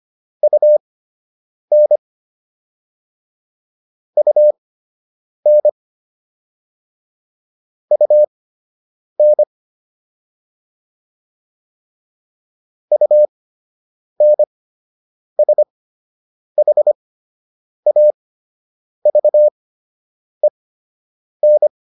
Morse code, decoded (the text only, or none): UN UN UN UNSHAVEN